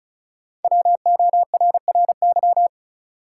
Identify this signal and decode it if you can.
Morse code, heard as a keyed tone, about 36 wpm, 700 Hz. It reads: WORRY
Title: Sets of 5 Words - New Word Drill - 35wpm